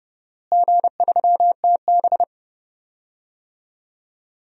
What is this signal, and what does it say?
Morse code, heard as a keyed tone, about 31 wpm, 700 Hz. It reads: G3TB
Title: DXCC-100 Call Signs - 4 Characters - Repeated 2x - 30wpm